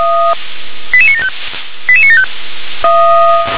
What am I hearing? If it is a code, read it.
Five-tone selective call (ZVEI), heard as quick runs of five tones, tones of about 70 ms. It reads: 88975 88975
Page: SDR/ZVEI